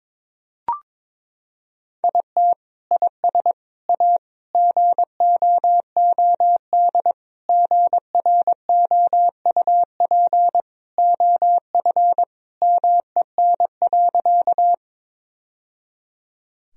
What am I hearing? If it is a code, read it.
Morse code, heard as a keyed tone, about 21 wpm, 700 Hz. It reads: IT IS A GOOD GROUP OF MEN.
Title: Sentences from Top 1000 Words - Review Part 1 - 22wpm